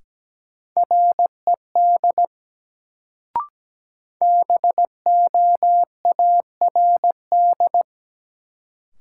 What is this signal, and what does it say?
Morse code, heard as a keyed tone, about 17 wpm, 700 Hz. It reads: RED BOARD
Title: Sentences from Top 700 Words - New Word Drill - 17wpm